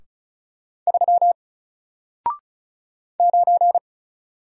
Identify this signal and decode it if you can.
Morse code, heard as a keyed tone, about 36 wpm, 700 Hz. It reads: 3 9
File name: Single-Letter-Number-35wpm